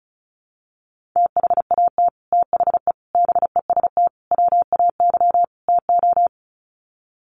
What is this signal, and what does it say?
Morse code, heard as a keyed tone, about 34 wpm, 700 Hz. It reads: THAT THE BEST WAY TO
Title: Sets of 5 Words Encore2 35wpm